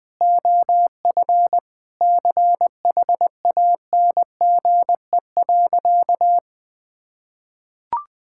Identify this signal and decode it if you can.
Morse code, heard as a keyed tone, about 20 wpm, 700 Hz. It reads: OF CHANGE.
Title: Sentences from Top 1000 Words - Review Part 5 - 20wpm